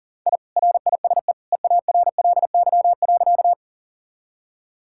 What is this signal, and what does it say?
Morse code, heard as a keyed tone, about 37 wpm, 700 Hz. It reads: I RISE EARLY.
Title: Sentences from Top 1000 Words - Review Part 6 - 40wpm